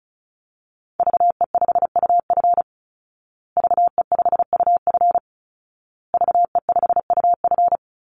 Morse code, heard as a keyed tone, about 34 wpm, 700 Hz. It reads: VE5UF VE5UF VE5UF
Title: Call Signs Repeated 2x - 35wpm